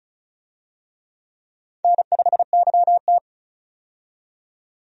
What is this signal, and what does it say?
Morse code, heard as a keyed tone, about 34 wpm, 700 Hz. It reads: N5YT
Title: USA Call Signs - 4 Characters - 35wpm